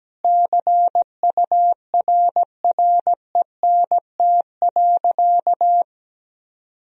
Morse code, heard as a keyed tone, about 17 wpm, 700 Hz. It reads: CURRENT.
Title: Sentences from Top 1000 Words - Review Part 4 - 17wpm